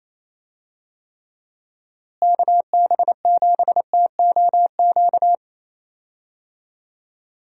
Morse code, heard as a keyed tone, about 28 wpm, 700 Hz. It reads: KB7TOQ
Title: USA Call Signs - 6 Characters - Repeated 2x - 28wpm